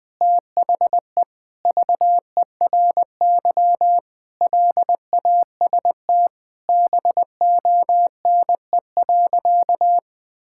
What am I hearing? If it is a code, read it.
Morse code, heard as a keyed tone, about 20 wpm, 700 Hz. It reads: THE VERY LAST BONE.
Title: Sentences from Top 1600 Words 20wpm